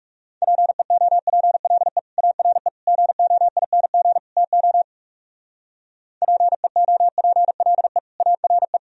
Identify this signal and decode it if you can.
Morse code, heard as a keyed tone, about 36 wpm, 700 Hz. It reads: PEOPLE ARE GOING TO PEOPLE ARE
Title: Speed-Racing - Sets of 4 Words 30wpm